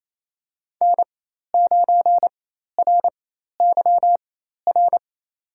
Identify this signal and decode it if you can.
Morse code, heard as a keyed tone, about 28 wpm, spaced as for 13 wpm, 700 Hz. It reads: N9RYR
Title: USA Call Signs - 5 Characters - 4x Character Spacing - 28wpm